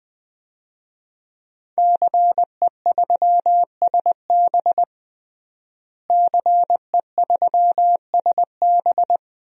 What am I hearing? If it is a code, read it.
Morse code, heard as a keyed tone, about 21 wpm, 700 Hz. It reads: CE3SB CE3SB
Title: DXCC-100 Call Signs - 5 Characters - Repeated 2x - 20wpm